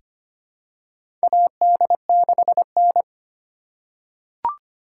Morse code, heard as a keyed tone, about 25 wpm, 700 Hz. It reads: AD6N
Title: USA Call Signs - 4 Characters - 3x Character Spacing - 25wpm